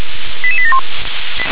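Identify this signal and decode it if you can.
Five-tone selective call (ZVEI), heard as quick runs of five tones, tones of about 70 ms. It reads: 88871
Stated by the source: ZVEI five-tone selective call received with an SDR.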